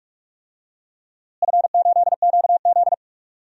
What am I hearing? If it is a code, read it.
Morse code, heard as a keyed tone, about 46 wpm, 700 Hz. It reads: F8QZ
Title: DXCC-100 Call Signs - 4 Characters - Repeated 2x - 45wpm